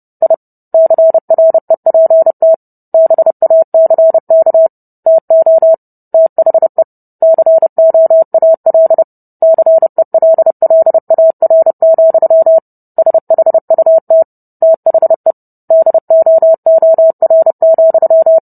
Morse code, heard as a keyed tone, about 30 wpm, 650 Hz. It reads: I CREPT BACK TO THE COAL CELLAR, SHUT THE DOOR,